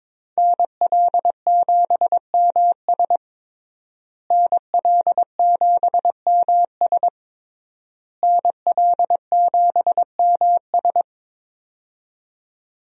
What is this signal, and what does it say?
Morse code, heard as a keyed tone, about 22 wpm, 700 Hz. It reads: NL7MS NL7MS NL7MS
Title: USA Call Signs - 5 Characters - Repeated 3x - 22wpm